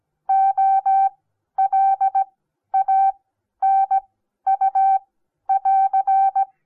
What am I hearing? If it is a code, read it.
Morse code, heard as a keyed tone, about 17 wpm, spaced as for 11 wpm, 780 Hz. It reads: OLANU <AR>
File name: Onlinelehrgang / morsen / DJ4UF Morsen08